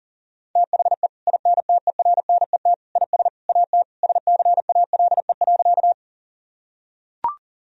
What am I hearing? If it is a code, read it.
Morse code, heard as a keyed tone, about 37 wpm, 690 Hz. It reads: THE INTERNET IS AT SCALE.
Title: Sentences from Top 1000 Words - Review Part 4 - 40wpm